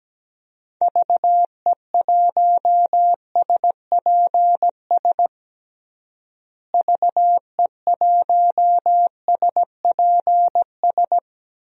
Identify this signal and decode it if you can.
Morse code, heard as a keyed tone, about 17 wpm, 700 Hz. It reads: VE1SPS VE1SPS
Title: DXCC-100 Call Signs - 6 Characters - Repeated 2x - 17wpm